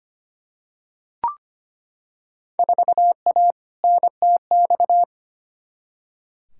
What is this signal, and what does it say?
Morse code, heard as a keyed tone, about 25 wpm, 700 Hz. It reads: 4A NTX
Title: Field Day Exchanges - 25wpm